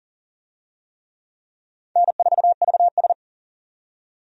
Morse code, heard as a keyed tone, about 39 wpm, 700 Hz. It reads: N4VS
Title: USA Call Signs - 4 Characters - 40wpm